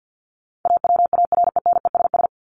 Morse code, heard as a keyed tone, about 48 wpm, 700 Hz. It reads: AWARENESS
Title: Top 1501-1750 Words 50wpm